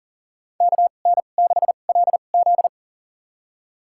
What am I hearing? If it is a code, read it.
Morse code, heard as a keyed tone, about 40 wpm, spaced as for 29 wpm, 700 Hz. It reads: KN6LZ
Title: USA Call Signs - 2x Character Spacing - 40wpm